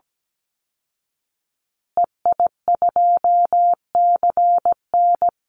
Morse code, heard as a keyed tone, about 17 wpm, 700 Hz. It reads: EI2CN
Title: Call Signs Repeated 3x - 17wpm